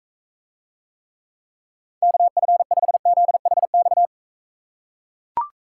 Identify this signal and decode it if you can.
Morse code, heard as a keyed tone, about 42 wpm, 700 Hz. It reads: KF5ZHX